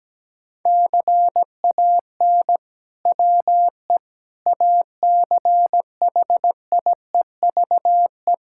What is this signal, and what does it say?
Morse code, heard as a keyed tone, about 17 wpm, 700 Hz. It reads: CAN WE ACHIEVE
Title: Sentences from Top 1000 Words - Review Part 6 - 17wpm